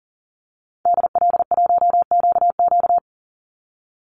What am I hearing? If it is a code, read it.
Morse code, heard as a keyed tone, about 39 wpm, 700 Hz. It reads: DL1QQ
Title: Call Signs Repeated 3x - 40wpm